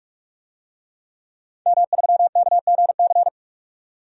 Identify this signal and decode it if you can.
Morse code, heard as a keyed tone, about 46 wpm, 700 Hz. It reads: M3KGC
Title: DXCC-100 Call Signs - 5 Characters - 45wpm